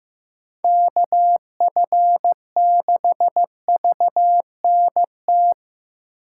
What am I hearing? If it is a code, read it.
Morse code, heard as a keyed tone, about 18 wpm, 700 Hz. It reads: KF6VNT
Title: Speed Racing - USA 2x3 Call Signs 15wpm